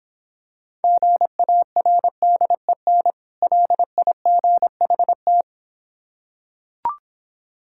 Morse code, heard as a keyed tone, about 26 wpm, 700 Hz. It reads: GARDEN LIGHT